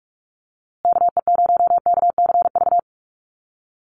Morse code, heard as a keyed tone, about 44 wpm, 700 Hz. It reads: KE0KCV
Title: USA Call Signs Repeated 3x 45wpm